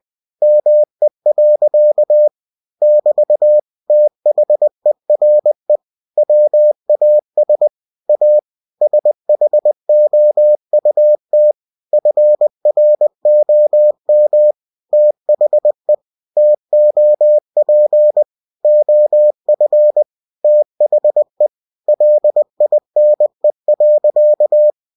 Morse code, heard as a keyed tone, about 20 wpm, 600 Hz. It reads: ME. <BT> THERE WAS A SHOUT FROM THE TOP OF THE LINE.
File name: cw-books / winnie-the-pooh / winnie-20wpm-page-0080